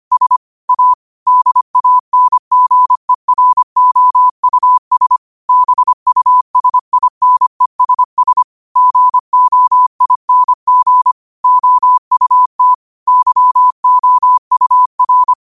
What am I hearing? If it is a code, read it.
Morse code, heard as a keyed tone, about 25 wpm, 1000 Hz. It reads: S A DANGEROUS BUSINESS GOING OUT YOUR
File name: qotdfiles / 25wpm